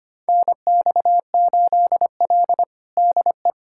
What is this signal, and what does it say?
Morse code, heard as a keyed tone, about 25 wpm, 700 Hz. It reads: NX8L DE
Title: USA Callsign DE Callsign - 4 Characters - 25wpm